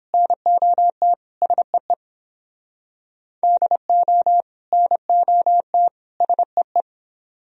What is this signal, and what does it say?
Morse code, heard as a keyed tone, about 30 wpm, 700 Hz. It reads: NOT SEE DO NOT SEE